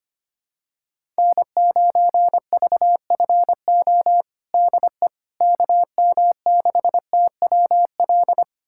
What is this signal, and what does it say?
Morse code, heard as a keyed tone, about 25 wpm, 700 Hz. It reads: N9VFO DE KM6TWL
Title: USA Callsign DE Callsign 25wpm